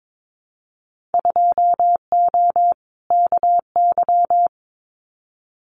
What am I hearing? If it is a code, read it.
Morse code, heard as a keyed tone, about 22 wpm, 700 Hz. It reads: 2O KY
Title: Winter Field Day Exchanges 22wpm